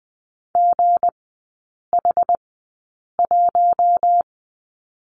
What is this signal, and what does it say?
Morse code, heard as a keyed tone, about 20 wpm, spaced as for 8 wpm, 690 Hz. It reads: GH1
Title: 159 Lesson 26e - Letter g - Three Characters 20wpm